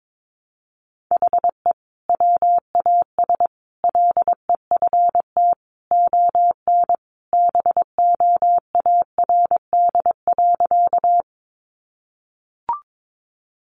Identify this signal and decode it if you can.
Morse code, heard as a keyed tone, about 22 wpm, 700 Hz. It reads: HE WAS LEFT ON BOARD.
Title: Sentences from Top 700 Words 22wpm